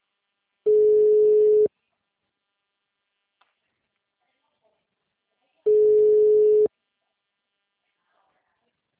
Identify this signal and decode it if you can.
Ringback tone (Europe, 425 Hz).
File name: out- -2016- - - .61411